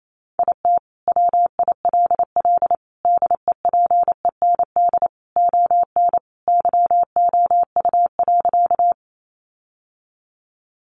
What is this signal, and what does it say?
Morse code, heard as a keyed tone, about 28 wpm, 700 Hz. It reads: IT WILL DEPEND ON YOU.
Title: Sentences from Top 1300 Words 28wpm